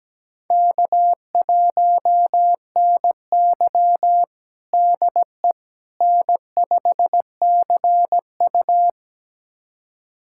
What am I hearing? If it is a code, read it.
Morse code, heard as a keyed tone, about 17 wpm, 700 Hz. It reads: K1NY DE N5CU
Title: USA Callsign DE Callsign - 4 Characters - 17wpm